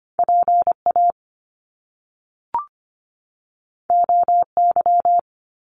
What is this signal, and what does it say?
Morse code, heard as a keyed tone, about 25 wpm, 700 Hz. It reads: PA OY
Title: Top Two Letter Combinations in Words 25wpm